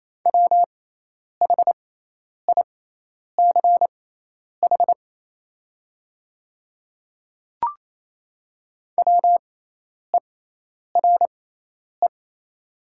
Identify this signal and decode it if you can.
Morse code, heard as a keyed tone, about 28 wpm, spaced as for 10 wpm, 700 Hz. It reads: WHICH WERE
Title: 6x Character Spacing - Top 200 Words - 28wpm